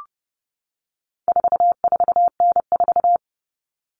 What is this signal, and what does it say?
Morse code, heard as a keyed tone, about 30 wpm, 700 Hz. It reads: 44N4
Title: All Cut Numbers - 4-Digit 30wpm